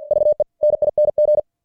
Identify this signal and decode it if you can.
Morse code, heard as a keyed tone, about 45 wpm, 600 Hz. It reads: VENING
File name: archived / CWBros